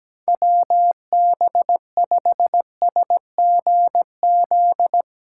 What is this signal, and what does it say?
Morse code, heard as a keyed tone, about 17 wpm, 700 Hz. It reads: WB5SGZ
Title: USA Callsign DE Callsign 17wpm